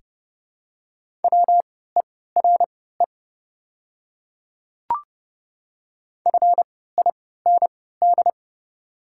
Morse code, heard as a keyed tone, about 30 wpm, spaced as for 17 wpm, 690 Hz. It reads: WERE FIND
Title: 3x Character Spacing - Top 100 Words - 30wpm